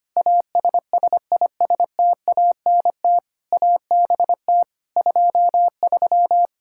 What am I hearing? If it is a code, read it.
Morse code, heard as a keyed tone, about 25 wpm, 700 Hz. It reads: ASSISTANT ABT 23
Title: Practice QSO - Exchange 3 - More Info - 25wpm